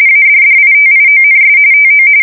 RTTY, about 45.45 baud. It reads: ADDITIONS TO